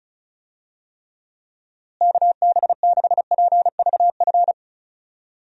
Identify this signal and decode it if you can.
Morse code, heard as a keyed tone, about 36 wpm, 700 Hz. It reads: KB6PVF
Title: USA Call Signs - 6 Characters - Repeated 2x - 35wpm